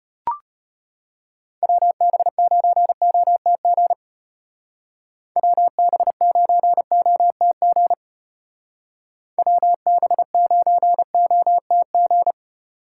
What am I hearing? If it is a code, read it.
Morse code, heard as a keyed tone, about 30 wpm, 700 Hz. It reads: WB9OTG WB9OTG WB9OTG
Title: Speed Racing - USA 2x3 Call Signs 25wpm